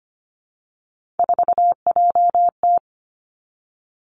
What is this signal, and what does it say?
Morse code, heard as a keyed tone, about 25 wpm, 700 Hz. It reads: 4JT
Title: Three Challenging Letter-Number Combinations 25wpm